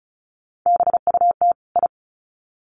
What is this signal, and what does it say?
Morse code, heard as a keyed tone, about 34 wpm, 700 Hz. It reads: BUT I
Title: Sets of 2 Words 35wpm